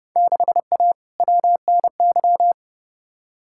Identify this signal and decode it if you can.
Morse code, heard as a keyed tone, about 29 wpm, 700 Hz. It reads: 6A WNY
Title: Field Day Exchanges Encore - 30wpm